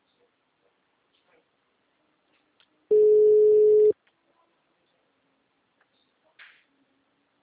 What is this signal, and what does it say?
Ringback tone (Europe, 425 Hz).
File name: out- -2020- - - .30483